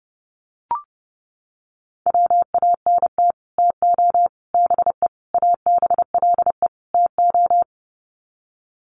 Morse code, heard as a keyed tone, about 30 wpm, 700 Hz. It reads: WANT TO BE ABLE TO
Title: Sets of 5 Words 30wpm